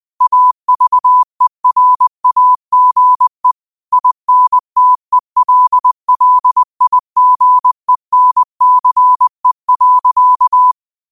Morse code, heard as a keyed tone, about 20 wpm, 1000 Hz. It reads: AVERAGE INTELLIGENCE.
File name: qotdfiles / 20wpm